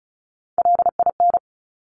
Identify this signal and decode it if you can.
Morse code, heard as a keyed tone, about 36 wpm, 700 Hz. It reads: LIN
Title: Top Three Letter Combinations in Words 35wpm